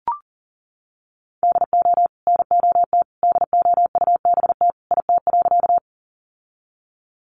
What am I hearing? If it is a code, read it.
Morse code, heard as a keyed tone, about 40 wpm, 700 Hz. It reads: DO NOT DOUBT IT.
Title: Sentences from Top 1500 Words 40wpm